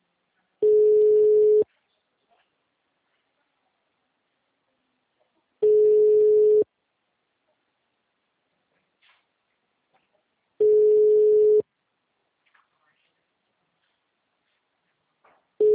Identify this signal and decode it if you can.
Ringback tone (Europe, 425 Hz).
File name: out- -2009- - - .123